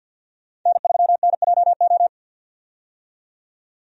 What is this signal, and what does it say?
Morse code, heard as a keyed tone, about 51 wpm, 700 Hz. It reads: N3NJO
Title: USA Call Signs - 5 Characters - 50wpm